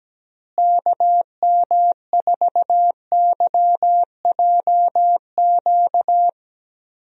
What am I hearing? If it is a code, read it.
Morse code, heard as a keyed tone, about 17 wpm, 700 Hz. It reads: KM4YJQ
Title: USA Call Signs - 6 Characters - Repeated 3x - 17wpm